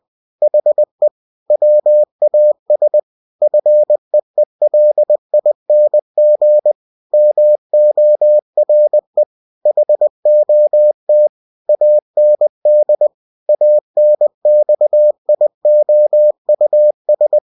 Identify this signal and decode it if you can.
Morse code, heard as a keyed tone, about 20 wpm, 600 Hz. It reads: HE WAS FEELING MORE HOT AND ANXIOUS
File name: cw-books / winnie-the-pooh / winnie-20wpm-page-0023